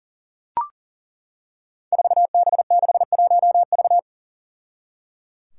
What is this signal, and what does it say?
Morse code, heard as a keyed tone, about 38 wpm, 700 Hz. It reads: 4B61V
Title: Five Challenging Letter-Number Combinations 40wpm